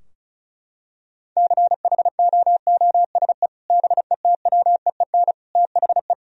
Morse code, heard as a keyed tone, about 36 wpm, 700 Hz. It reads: CHOOSE BETWEEN THE
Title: Sentences from Top 1000 Words - Review Part 4 - 35wpm